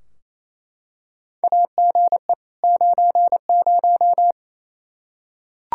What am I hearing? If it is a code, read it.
Morse code, heard as a keyed tone, about 28 wpm, 700 Hz. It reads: AGE 90
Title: Practice QSO - Exchange 3 - More Info - 28wpm